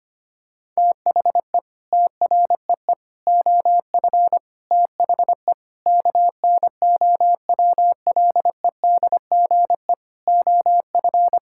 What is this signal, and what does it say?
Morse code, heard as a keyed tone, about 25 wpm, 700 Hz. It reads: THE TREE OF THE KNOWLEDGE OF
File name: practice / 25wpm / Sentences with 9-Word Phrases - 25wpm